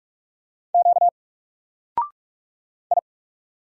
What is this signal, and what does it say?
Morse code, heard as a keyed tone, about 44 wpm, 700 Hz. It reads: Q I
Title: Single Letter-Number 45wpm